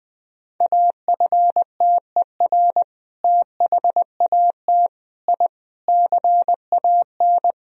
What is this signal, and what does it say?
Morse code, heard as a keyed tone, about 20 wpm, 700 Hz. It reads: AFTER THAT I CAN
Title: Sentences from Top 1000 Words - Review Part 2 - 20wpm